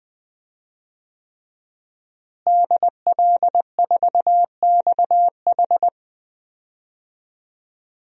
Morse code, heard as a keyed tone, about 20 wpm, 700 Hz. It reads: DL4XH
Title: DXCC-100 Call Signs - 5 Characters - Repeated 3x - 20wpm